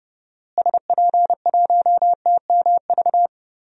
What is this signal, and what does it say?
Morse code, heard as a keyed tone, about 30 wpm, 700 Hz. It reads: SP1TMV
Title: DXCC-100 Call Signs - 6 Characters - Repeated 2x - 30wpm